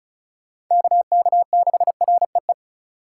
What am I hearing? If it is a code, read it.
Morse code, heard as a keyed tone, about 38 wpm, 700 Hz. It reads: KK6REE